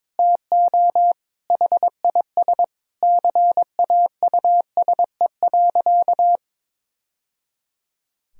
Morse code, heard as a keyed tone, about 22 wpm, 700 Hz. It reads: TO HIS CAUSE.
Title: Sentences from Top 1000 Words - Review Part 4 - 22wpm